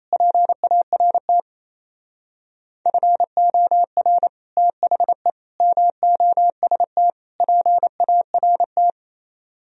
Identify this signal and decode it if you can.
Morse code, heard as a keyed tone, about 35 wpm, 700 Hz. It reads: PART FOR THE MOST PART
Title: Speed-Racing - Sets of 4 Words 28wpm